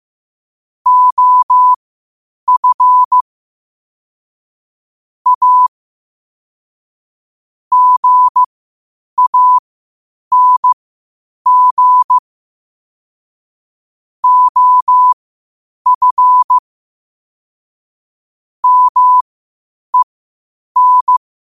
Morse code, heard as a keyed tone, about 15 wpm, spaced as for 8 wpm, 1000 Hz. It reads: OF A GANG OF MEN
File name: 8wpm